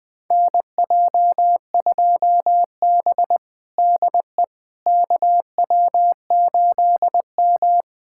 Morse code, heard as a keyed tone, about 21 wpm, 700 Hz. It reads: NJ2B DE KW8M
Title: USA Callsign DE Callsign - 4 Characters - 20wpm